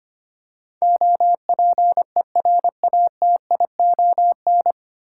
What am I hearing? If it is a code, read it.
Morse code, heard as a keyed tone, about 25 wpm, 700 Hz. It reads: OPERATION
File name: Nine Letter Words - 25wpm